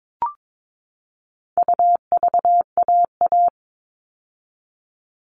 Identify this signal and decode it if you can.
Morse code, heard as a keyed tone, about 22 wpm, 700 Hz. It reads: UVAA
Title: All Cut Numbers - 4-Digit 22wpm